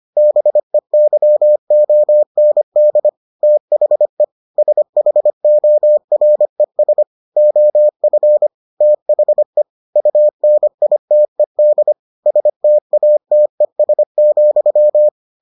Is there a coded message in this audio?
Yes: Morse code from 0.2 to 15.1 s, heard as a keyed tone, about 25 wpm, 600 Hz. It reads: BEYOND THE SHORES OF THE UNITED STATES,